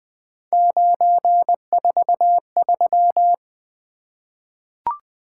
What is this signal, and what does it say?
Morse code, heard as a keyed tone, about 20 wpm, 700 Hz. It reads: 943
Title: Three Letter-Number Combinations - 20wpm